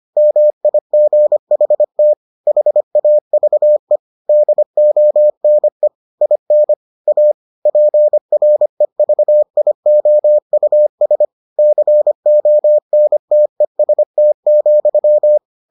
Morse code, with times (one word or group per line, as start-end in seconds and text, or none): 0.2-2.1 MIGHT
2.5-4.0 HAVE
4.3-5.9 DONE
6.2-6.7 IN
7.1-7.3 A
7.7-11.2 PREVIOUS
11.6-15.4 CONTEST,